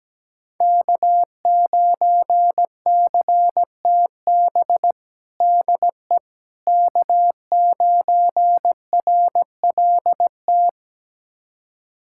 Morse code, heard as a keyed tone, about 17 wpm, 700 Hz. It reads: K9CTB DE K9RLT
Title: USA Callsign DE Callsign - 5 Characters - 17wpm